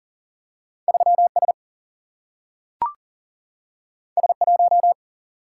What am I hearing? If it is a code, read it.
Morse code, heard as a keyed tone, about 40 wpm, 700 Hz. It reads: 3S S1
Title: Single-Letter Single-Number 40wpm